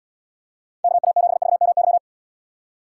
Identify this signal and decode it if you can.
Morse code, heard as a keyed tone, about 77 wpm, 700 Hz. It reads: KA7VMY